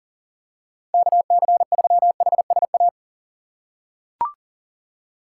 Morse code, heard as a keyed tone, about 40 wpm, 700 Hz. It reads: KC3HSA
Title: USA Call Signs - 6 Characters - 6x Character Spacing - 40wpm